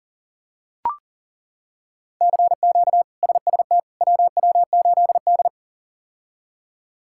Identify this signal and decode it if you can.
Morse code, heard as a keyed tone, about 40 wpm, 700 Hz. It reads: CQ SST WW8D
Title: K1USN SST QSO Practice - 40wpm